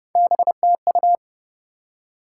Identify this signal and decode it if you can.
Morse code, heard as a keyed tone, about 32 wpm, 700 Hz. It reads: BTU
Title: Top 90 QSO Elements 30wpm